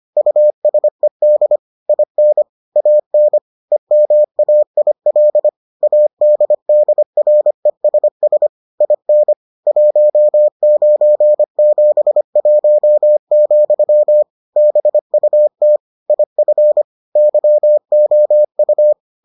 Morse code, heard as a keyed tone, about 25 wpm, 600 Hz. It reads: USED IN AN EMAIL ADDRESS IN 1971, BUT IF YOU